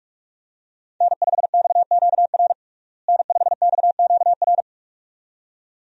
Morse code, heard as a keyed tone, about 45 wpm, 700 Hz. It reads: N5XQR N5XQR
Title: USA Call Signs - 5 Characters - Repeated 2x - 45wpm